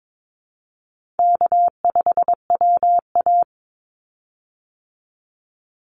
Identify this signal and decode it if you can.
Morse code, heard as a keyed tone, about 22 wpm, 700 Hz. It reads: K5WA
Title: Easy Call Signs Repeated 3x - 22wpm